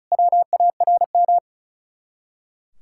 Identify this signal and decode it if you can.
Morse code, heard as a keyed tone, about 46 wpm, 700 Hz. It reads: WARM